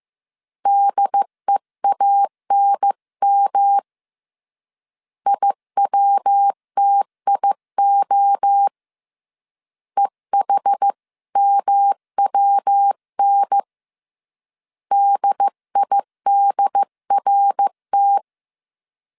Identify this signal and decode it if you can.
Morse code, heard as a keyed tone, about 15 wpm, 800 Hz. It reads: DEANM IWTIO EHMWN DIDRT
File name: CW / DET / 1 3 1500C S A 14wpm 2447